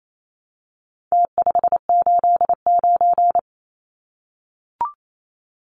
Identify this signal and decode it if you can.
Morse code, heard as a keyed tone, about 28 wpm, 700 Hz. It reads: T589
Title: Numbers - Contest Style - 4-Digit 28wpm